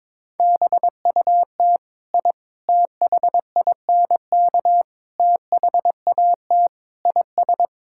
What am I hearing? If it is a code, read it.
Morse code, heard as a keyed tone, about 22 wpm, 700 Hz. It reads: BUT I THINK THAT IS
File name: Sentences with 4-Word Phrases - 22wpm